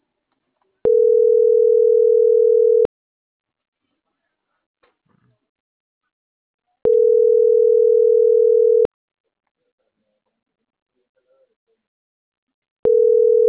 Ringback tone (North America), 440 + 480 Hz.